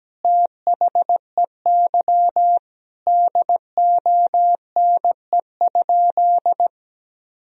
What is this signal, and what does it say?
Morse code, heard as a keyed tone, about 17 wpm, 700 Hz. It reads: THEY DONE?
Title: Sentences from Top 500 Words and Abbreviations 17wpm